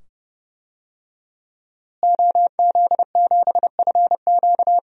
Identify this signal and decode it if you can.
Morse code, heard as a keyed tone, about 30 wpm, 700 Hz. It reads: OZ7FQ
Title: DXCC-100 Call Signs - 5 Characters - Repeated 2x - 30wpm